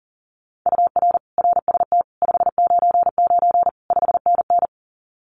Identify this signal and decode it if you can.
Morse code, heard as a keyed tone, about 40 wpm, 700 Hz. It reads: UR RST 599 5NN
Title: Common QSO Phrases Encore 40wpm